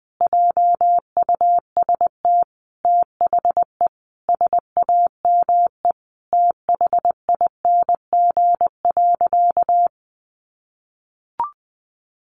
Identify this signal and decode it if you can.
Morse code, heard as a keyed tone, about 20 wpm, 700 Hz. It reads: JUST THE SAME THING.
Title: Sentences from Top 800 Words 20wpm